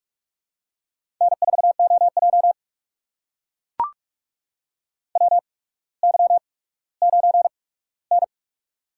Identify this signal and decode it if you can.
Morse code, heard as a keyed tone, about 44 wpm, spaced as for 19 wpm, 700 Hz. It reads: N4OJ WY9N
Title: USA Call Signs - 4 Characters - 8x Character Spacing - 45wpm